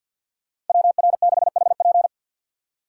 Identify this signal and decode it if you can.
Morse code, heard as a keyed tone, about 52 wpm, 700 Hz. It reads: WR6HP
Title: USA Call Signs - 5 Characters - 3x Character Spacing - 50wpm